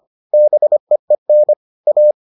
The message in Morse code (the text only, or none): BEEN A